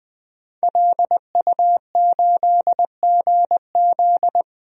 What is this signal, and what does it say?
Morse code, heard as a keyed tone, about 20 wpm, 700 Hz. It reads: LU8GZ
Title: DXCC-100 Call Signs - 5 Characters - 20wpm